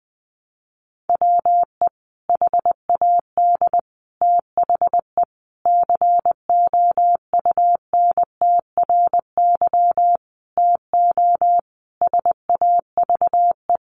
Morse code, heard as a keyed tone, about 19 wpm, 700 Hz. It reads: WE HAD THE COUNTRY TO SAVE
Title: Sentences from Top 1100 Words 20wpm